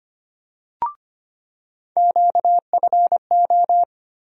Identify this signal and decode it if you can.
Morse code, heard as a keyed tone, about 25 wpm, 700 Hz. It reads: QFO
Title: USA Call Sign Postfixes - 3 Letters 25wpm